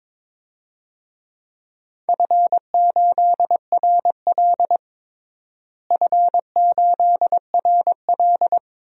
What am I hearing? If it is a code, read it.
Morse code, heard as a keyed tone, about 22 wpm, 700 Hz. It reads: F8RL F8RL
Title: DXCC-100 Call Signs - 4 Characters - Repeated 2x - 22wpm